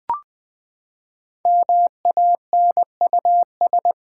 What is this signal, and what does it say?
Morse code, heard as a keyed tone, about 20 wpm, 700 Hz. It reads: MANUS
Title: Build A Word - 5 Letters At A Time - 10-Letter Words - 20wpm